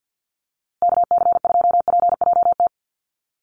Morse code, heard as a keyed tone, about 50 wpm, 700 Hz. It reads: KC2PJT
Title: USA Call Signs Repeated 2x 50wpm